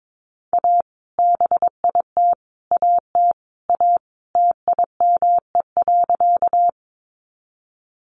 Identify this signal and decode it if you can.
Morse code, heard as a keyed tone, about 22 wpm, 700 Hz. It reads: A BIT AT A TIME.
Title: Sentences from Top 900 Words 22wpm